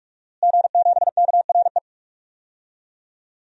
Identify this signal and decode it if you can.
Morse code, heard as a keyed tone, about 46 wpm, 700 Hz. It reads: G7KRE
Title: DXCC-100 Call Signs - 5 Characters - Repeated 3x - 45wpm